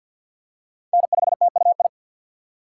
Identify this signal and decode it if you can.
Morse code, heard as a keyed tone, about 47 wpm, 700 Hz. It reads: N5TUI
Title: USA Call Signs - 5 Characters - 4x Character Spacing - 50wpm